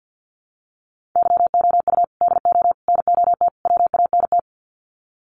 Morse code, heard as a keyed tone, about 50 wpm, 700 Hz. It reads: YOU DO NOT WANT
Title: Sets of 4 Words 50wpm